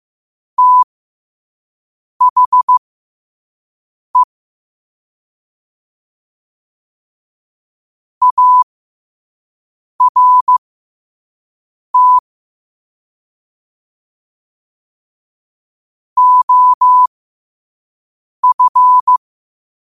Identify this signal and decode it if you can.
Morse code, heard as a keyed tone, about 15 wpm, spaced as for 5 wpm, 1000 Hz. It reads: THE ART OF
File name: qotdfiles / 5wpm